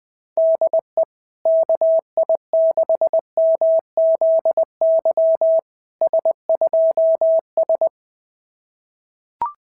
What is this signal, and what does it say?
Morse code, heard as a keyed tone, about 20 wpm, 690 Hz. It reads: DE KI6MZY S2S
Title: SOTA QSO Practice 20wpm